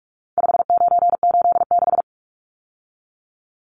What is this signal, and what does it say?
Morse code, heard as a keyed tone, about 44 wpm, 700 Hz. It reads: 5986
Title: Numbers - 4-Digit 45wpm